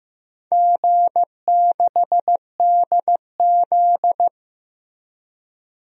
Morse code, heard as a keyed tone, about 15 wpm, 700 Hz. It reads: G6DZ
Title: DXCC-100 Call Signs - 4 Characters - Repeated 3x - 15wpm